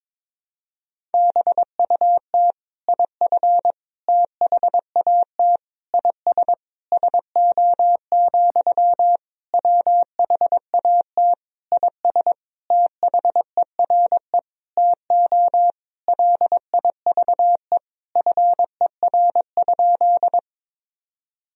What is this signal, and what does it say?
Morse code, heard as a keyed tone, about 22 wpm, 700 Hz. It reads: BUT IF THAT IS SO, WHAT IS THERE TO LIVE FER?
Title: Sentences from Top 200 Words and Abbreviations 22wpm